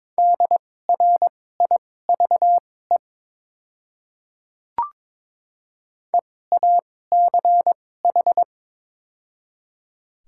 Morse code, heard as a keyed tone, about 21 wpm, spaced as for 16 wpm, 700 Hz. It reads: DRIVE EACH
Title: 2x Character Spacing - Top 500 Words - 22wpm